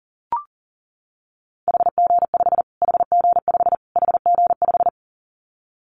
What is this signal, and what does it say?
Morse code, heard as a keyed tone, about 37 wpm, 700 Hz. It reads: HG5 HG5 HG5
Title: Callsign Prefixes for Top 25 DXCC - Repeated 3x - 40wpm